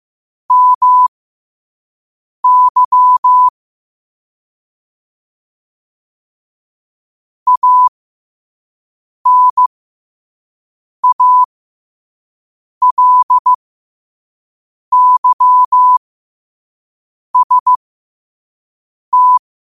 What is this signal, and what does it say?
Morse code, heard as a keyed tone, about 15 wpm, spaced as for 5 wpm, 1000 Hz. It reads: MY ANALYST
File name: qotdfiles / 5wpm